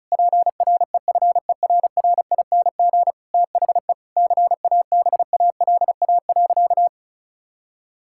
Morse code, heard as a keyed tone, about 36 wpm, 700 Hz. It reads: PREFERRING THE CABALA.
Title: Pangram Challenge v2 - 35wpm